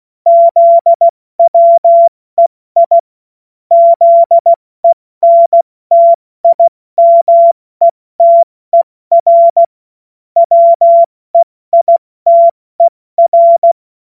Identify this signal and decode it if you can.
Morse code, heard as a keyed tone, about 16 wpm, 680 Hz. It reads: ZWEI ZENTIMETER WEITER
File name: Morsekurs / Witze / Witz 18. Juli 2023 70 F80